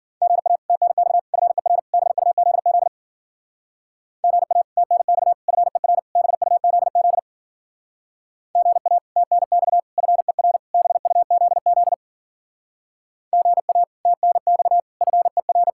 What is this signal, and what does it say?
Morse code, heard as a keyed tone, about 52 wpm, 700 Hz. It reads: GA TNX FER BUZZ GA TNX FER BUZZ GA TNX FER BUZZ GA TNX FER